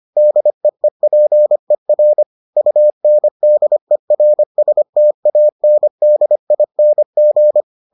Morse code, heard as a keyed tone, about 25 wpm, 600 Hz. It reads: DEEPER UNDERSTANDING